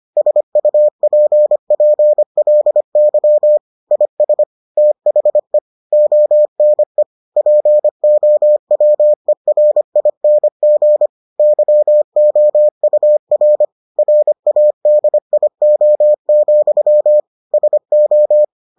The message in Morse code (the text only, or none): SUPPLY IS THE ONE POWERING YOUR RADIO, SO